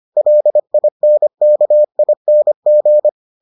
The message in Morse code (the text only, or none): LINKING